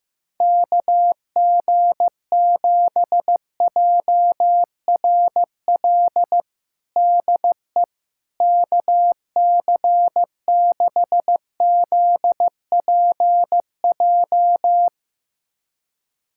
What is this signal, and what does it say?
Morse code, heard as a keyed tone, about 15 wpm, 700 Hz. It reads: KG7JRL DE KC6ZPJ
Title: USA Callsign DE Callsign - 6 Characters - 15wpm